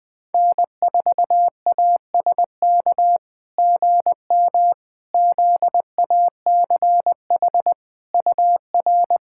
Morse code, heard as a keyed tone, about 20 wpm, 700 Hz. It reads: N4ASK GM ZACH UR
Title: SOTA QSO Practice 20wpm